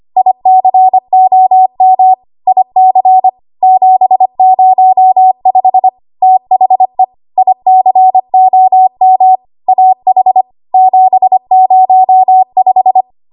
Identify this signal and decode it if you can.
Morse code, heard as a keyed tone, about 25 wpm, 750 Hz. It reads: ICOM IC 705 THE ICOM AH 705
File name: Archive / 25 WPM / 25WPM